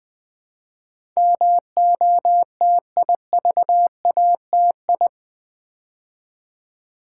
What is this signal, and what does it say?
Morse code, heard as a keyed tone, about 20 wpm, 700 Hz. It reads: MOTIVATI
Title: Build A Word - 4 Letters At A Time - 10-Letter Words - 20wpm